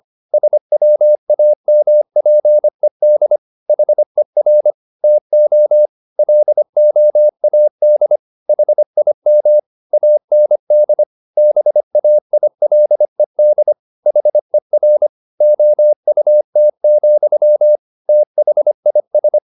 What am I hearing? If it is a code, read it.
Morse code, heard as a keyed tone, about 25 wpm, 600 Hz. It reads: SWAMPED HER TO LOAD HIM AND BAILED HER OUT, THIS